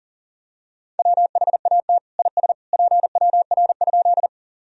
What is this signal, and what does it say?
Morse code, heard as a keyed tone, about 40 wpm, 700 Hz. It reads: WHAT IS PWR?
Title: Sentences from Top 500 Words and Abbreviations 40wpm